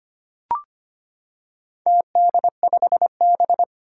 Morse code, heard as a keyed tone, about 25 wpm, 700 Hz. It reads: TD5B
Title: Four Challenging Letter-Number Combinations 25wpm